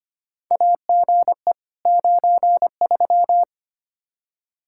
Morse code, heard as a keyed tone, about 25 wpm, 700 Hz. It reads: AGE 93
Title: Practice QSO - Exchange 3 - More Info - 25wpm